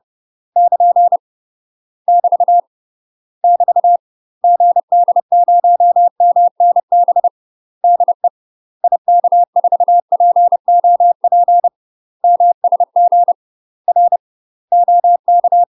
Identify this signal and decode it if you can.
Morse code, heard as a keyed tone, about 30 wpm, 700 Hz. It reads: <KN> <BT> <BT> GD0MNB DE IK4POP MSG R OK
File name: UK MASTER QSOs TRAINING / 30WPM / UK MASTER QSOs 0017